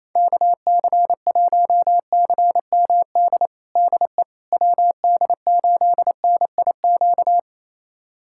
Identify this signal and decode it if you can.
Morse code, heard as a keyed tone, about 28 wpm, 700 Hz. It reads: KC1CMD DE WD8NIQ
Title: USA Callsign DE Callsign 28wpm